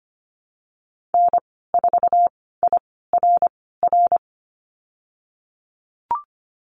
Morse code, heard as a keyed tone, about 25 wpm, spaced as for 16 wpm, 700 Hz. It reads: N4IRR
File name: Lesson 15g - Number 5 - Call Signs 25wpm